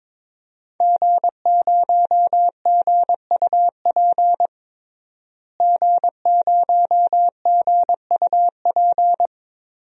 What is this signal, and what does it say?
Morse code, heard as a keyed tone, about 22 wpm, 700 Hz. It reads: G0GUP G0GUP
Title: DXCC-100 Call Signs - 5 Characters - Repeated 3x - 22wpm